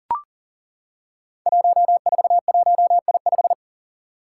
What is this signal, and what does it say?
Morse code, heard as a keyed tone, about 38 wpm, 700 Hz. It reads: 141I5
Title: Five Challenging Letter-Number Combinations 40wpm